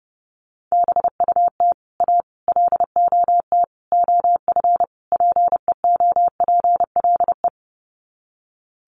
Morse code, heard as a keyed tone, about 30 wpm, 700 Hz. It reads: BUT A LOT OF PEOPLE
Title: Sets of 5 Words Encore2 30wpm